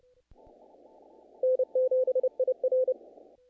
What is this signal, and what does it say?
Morse code, heard as a keyed tone, about 28 wpm, 510 Hz. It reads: N7IR
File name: contests / naqp-cw-k5tr-op-k5pi- -1800